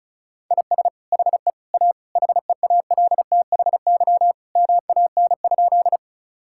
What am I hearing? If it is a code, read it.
Morse code, heard as a keyed tone, about 34 wpm, 700 Hz. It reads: IS HE A HEALTHY MAN?
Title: Sentences from Top 1000 Words - Review Part 5 - 35wpm